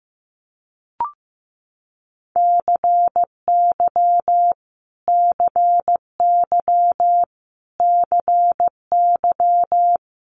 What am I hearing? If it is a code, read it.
Morse code, heard as a keyed tone, about 15 wpm, 700 Hz. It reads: CY CY CY
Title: Multi-letter Phonemes Repeated 3x 15wpm